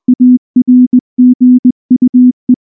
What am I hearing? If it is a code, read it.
Morse code, heard as a keyed tone, about 19 wpm, 250 Hz. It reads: ARGUE
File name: morse / morse-training-20WPM-4